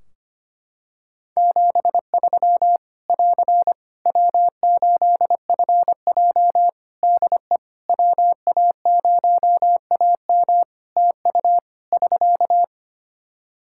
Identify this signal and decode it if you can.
Morse code, heard as a keyed tone, about 25 wpm, 700 Hz. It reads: 73 <AR> W8FJ DE WA0AM TU <SK>
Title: Practice QSO - Exchange 4 - Ending - 25wpm